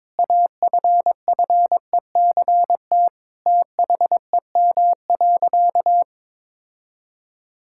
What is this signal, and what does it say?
Morse code, heard as a keyed tone, about 22 wpm, 700 Hz. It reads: AFFECT THEM.
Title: Sentences from Top 1000 Words - Review Part 6 - 22wpm